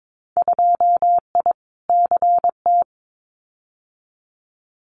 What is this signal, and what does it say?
Morse code, heard as a keyed tone, about 22 wpm, 700 Hz. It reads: 2I CT
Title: Winter Field Day Exchanges 22wpm